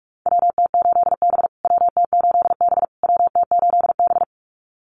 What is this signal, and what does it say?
Morse code, heard as a keyed tone, about 44 wpm, 700 Hz. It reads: WT8B WT8B WT8B
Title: USA Call Signs Repeated 3x 45wpm